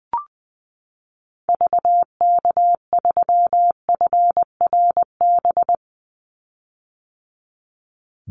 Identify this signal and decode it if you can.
Morse code, heard as a keyed tone, about 20 wpm, 700 Hz. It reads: VK3FRB
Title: USA Call Signs 20wpm